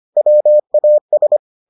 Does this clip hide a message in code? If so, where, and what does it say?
Yes: Morse code from 0.2 to 1.4 s, heard as a keyed tone, about 25 wpm, 600 Hz. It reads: WAS